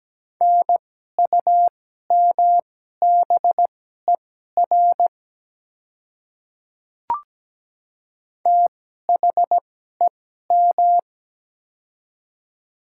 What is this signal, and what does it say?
Morse code, heard as a keyed tone, about 17 wpm, spaced as for 12 wpm, 700 Hz. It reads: NUMBER THEM
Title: 2x Character Spacing - Top 100 Words - 17wpm